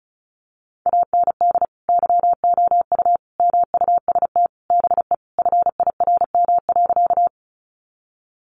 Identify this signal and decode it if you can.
Morse code, heard as a keyed tone, about 34 wpm, 700 Hz. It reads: AND YOU MUST BE FIRM.
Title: Sentences from Top 1400 Words 35wpm